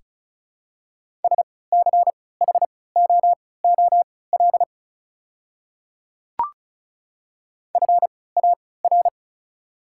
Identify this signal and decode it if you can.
Morse code, heard as a keyed tone, about 34 wpm, spaced as for 20 wpm, 690 Hz. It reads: SCHOOL FAR
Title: 3x Character Spacing - Top 300 Words - 35wpm